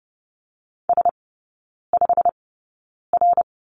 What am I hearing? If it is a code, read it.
Morse code, heard as a keyed tone, about 30 wpm, spaced as for 9 wpm, 700 Hz. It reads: S5R
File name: Lesson 17e - Letter u - Three Characters 30wpm